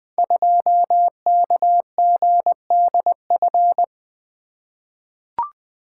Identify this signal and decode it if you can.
Morse code, heard as a keyed tone, about 20 wpm, 700 Hz. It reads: 2KGDF
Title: Code Groups - 5 Letter:Numbers - 20wpm